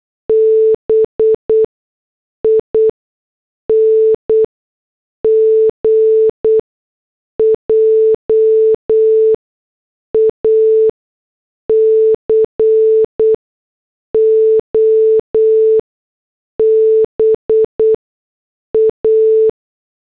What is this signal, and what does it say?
Morse code, heard as a keyed tone, about 8 wpm, spaced as for 6 wpm, 440 Hz. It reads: BING JACOBA